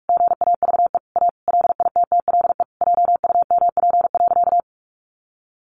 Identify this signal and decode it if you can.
Morse code, heard as a keyed tone, about 44 wpm, 700 Hz. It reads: GAVE A LITTLE JUMP.
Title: Sentences from Top 1300 Words 45wpm